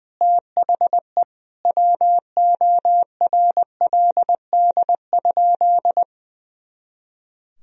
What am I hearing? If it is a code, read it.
Morse code, heard as a keyed tone, about 20 wpm, 700 Hz. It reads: THE WORLD?
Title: Sentences from Top 200 Words and Abbreviations 20wpm